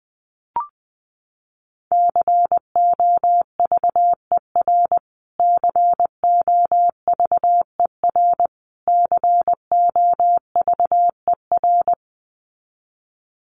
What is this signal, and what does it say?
Morse code, heard as a keyed tone, about 20 wpm, 700 Hz. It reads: COVER COVER COVER
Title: Top 300 Words Repeated 20wpm